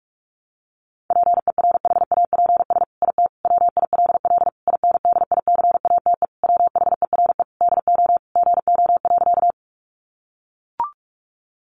Morse code, heard as a keyed tone, about 44 wpm, 700 Hz. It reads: PERHAPS IT WILL INDICATE WHERE DO GO.
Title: Sentences from Top 1300 Words 45wpm